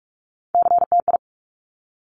Morse code, heard as a keyed tone, about 46 wpm, 690 Hz. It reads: CTI
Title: Top Three Letter Combinations in Words 45wpm